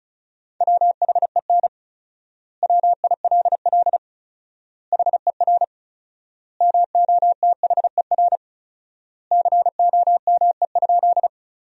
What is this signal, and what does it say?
Morse code, heard as a keyed tone, about 36 wpm, 700 Hz. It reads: WHEN WILL HER MOTHER COME?
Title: 4x Word-Spacing - Sentences from Top 200 Words - 35wpm